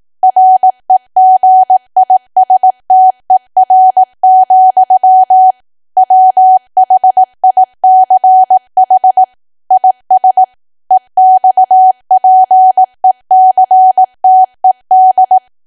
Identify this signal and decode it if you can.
Morse code, heard as a keyed tone, about 18 wpm, 750 Hz. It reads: REGISTER, WHICH IS EXPECTED
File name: cwa / W1AW-Level2 / 18WPM